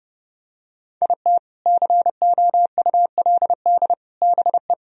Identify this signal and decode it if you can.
Morse code, heard as a keyed tone, about 29 wpm, 700 Hz. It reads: IT COULD BE